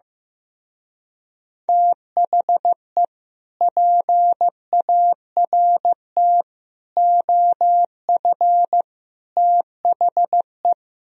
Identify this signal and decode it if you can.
Morse code, heard as a keyed tone, about 15 wpm, 700 Hz. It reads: THE PART OF THE
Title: Speed-Racing - Sets of 4 Words 15wpm